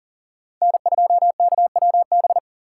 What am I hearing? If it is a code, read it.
Morse code, heard as a keyed tone, about 42 wpm, 700 Hz. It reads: N2KWB
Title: USA Call Signs - 5 Characters - 6x Character Spacing - 40wpm